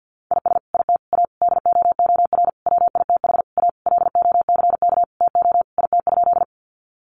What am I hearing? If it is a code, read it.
Morse code, heard as a keyed tone, about 50 wpm, 700 Hz. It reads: IS IT A DOOR WITH A LOCK TO IT?
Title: Sentences from Top 1300 Words 50wpm